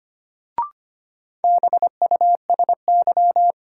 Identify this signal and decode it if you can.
Morse code, heard as a keyed tone, about 25 wpm, 700 Hz. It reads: BUSY
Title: Sentences from Top 1000 Words - Review Part 2 - New Word Drill - 25wpm